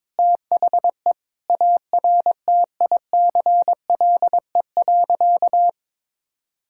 Morse code, heard as a keyed tone, about 21 wpm, 700 Hz. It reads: THE ARTICLE.
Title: Sentences from Top 1000 Words - Review Part 5 - 22wpm